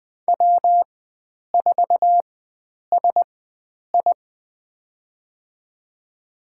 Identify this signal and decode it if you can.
Morse code, heard as a keyed tone, about 20 wpm, spaced as for 9 wpm, 700 Hz. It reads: W4SI
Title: USA Call Signs - 4 Characters - 4x Character Spacing - 20wpm